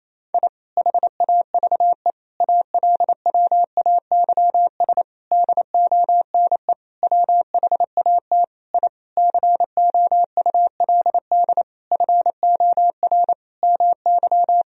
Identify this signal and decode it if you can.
Morse code, heard as a keyed tone, about 28 wpm, 700 Hz. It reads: I HAVE ALWAYS DONE WHAT I COULD FOR MY
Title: Sentences from Top 1000 Words - Review Part 2 - 28wpm